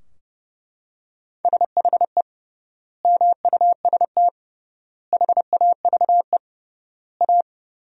Morse code, heard as a keyed tone, about 32 wpm, 700 Hz. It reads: SHE MUST HAVE A
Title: 3x Word-Spacing - Sentences from Top 300 Words - 30wpm